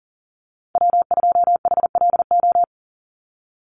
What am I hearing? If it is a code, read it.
Morse code, heard as a keyed tone, about 38 wpm, 690 Hz. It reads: W2HLO
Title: USA Call Signs 40wpm